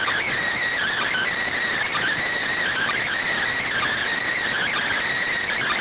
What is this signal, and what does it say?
Emergency Alert System (SAME) tones (header bursts), heard as harsh buzzing bursts. It is heard from 0.7 to 5.0 s.